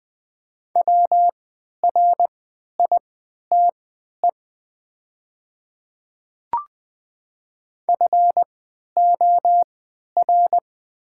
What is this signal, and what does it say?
Morse code, heard as a keyed tone, about 20 wpm, spaced as for 11 wpm, 700 Hz. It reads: WRITE FOR
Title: 3x Character Spacing - Top 300 Words - 20wpm